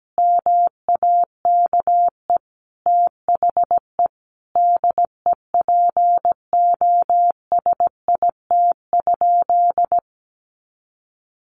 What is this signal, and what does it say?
Morse code, heard as a keyed tone, about 17 wpm, 700 Hz. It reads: MAKE THE DEPOSIT?
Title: Sentences from Top 1300 Words 17wpm